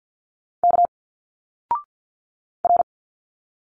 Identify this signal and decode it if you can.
Morse code, heard as a keyed tone, about 49 wpm, 700 Hz. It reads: K R
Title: Single Letters 50wpm